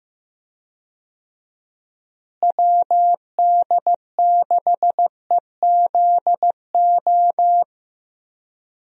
Morse code, heard as a keyed tone, about 15 wpm, 700 Hz. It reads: WD6EZO
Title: USA Call Signs - 6 Characters - Repeated 3x - 15wpm